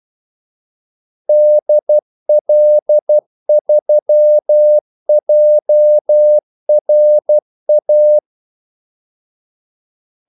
Morse code, heard as a keyed tone, about 12 wpm, 600 Hz. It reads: DL3JRA
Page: 2 ekstra ordmellemrum mellem kaldesignalerne